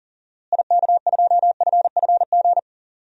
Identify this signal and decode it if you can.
Morse code, heard as a keyed tone, about 40 wpm, 700 Hz. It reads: IK2FFG
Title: DXCC-100 Call Signs - 6 Characters - Repeated 2x - 40wpm